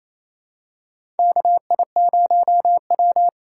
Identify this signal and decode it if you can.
Morse code, heard as a keyed tone, about 28 wpm, 700 Hz. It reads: KI0W
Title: USA Callsign DE Callsign - 4 Characters - 28wpm